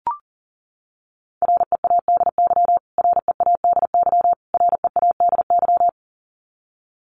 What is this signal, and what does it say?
Morse code, heard as a keyed tone, about 40 wpm, 700 Hz. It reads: READY READY READY
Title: Top 500 Words Repeated 40wpm